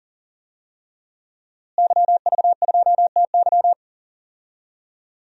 Morse code, heard as a keyed tone, about 38 wpm, 700 Hz. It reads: YV2TY
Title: DXCC-100 Call Signs - 5 Characters - Repeated 3x - 40wpm